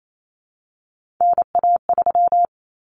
Morse code, heard as a keyed tone, about 28 wpm, 700 Hz. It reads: NA3
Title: Common Cut Numbers - 3-Digit 28wpm